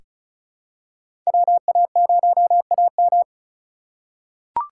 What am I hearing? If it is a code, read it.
Morse code, heard as a keyed tone, about 36 wpm, 700 Hz. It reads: WA0AM
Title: USA Call Signs - 4x Character Spacing - 35wpm